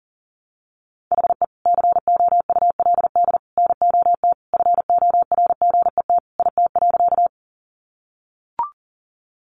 Morse code, heard as a keyed tone, about 40 wpm, 700 Hz. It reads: HE COULD NOT FORGET IT.
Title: Sentences from Top 1200 Words 40wpm